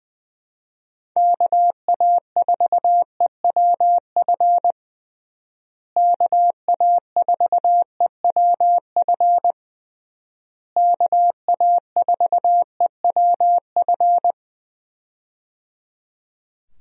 Morse code, heard as a keyed tone, about 20 wpm, 700 Hz. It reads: KA4EWF KA4EWF KA4EWF
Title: USA Call Signs - 6 Characters - Repeated 3x - 20wpm